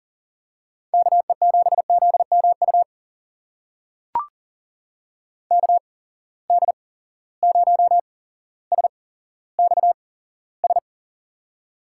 Morse code, heard as a keyed tone, about 39 wpm, spaced as for 17 wpm, 700 Hz. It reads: KE7ZMU KD0SXS
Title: USA Call Signs - 6 Characters - 8x Character Spacing - 40wpm